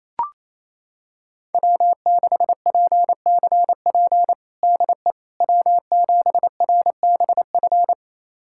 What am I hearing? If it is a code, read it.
Morse code, heard as a keyed tone, about 28 wpm, 700 Hz. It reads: W6PCP DE W7RBF
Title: USA Callsign DE Callsign - 5 Characters - 28wpm